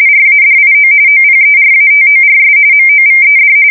RTTY, about 45.45 baud. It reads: RADIO OPERATORS MOVED
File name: fsk170 / teletype-RTTY